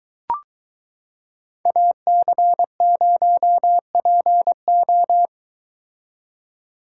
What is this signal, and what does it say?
Morse code, heard as a keyed tone, about 22 wpm, 700 Hz. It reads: AC0PO